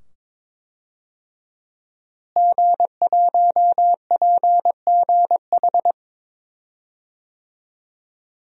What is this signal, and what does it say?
Morse code, heard as a keyed tone, about 23 wpm, 700 Hz. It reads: G1PGH
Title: DXCC-100 Call Signs - 5 Characters - Repeated 2x - 22wpm